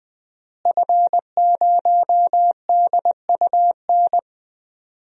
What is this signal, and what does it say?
Morse code, heard as a keyed tone, about 20 wpm, 700 Hz. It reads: F0DUN
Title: DXCC-100 Call Signs - 5 Characters - Repeated 2x - 20wpm